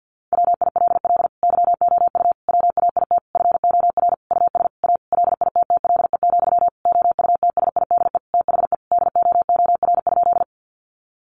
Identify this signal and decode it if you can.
Morse code, heard as a keyed tone, about 48 wpm, 700 Hz. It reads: WILL YOU WAIT FOR US A LITTLE, OUTSIDE THE DOOR?
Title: Sentences from Top 900 Words 50wpm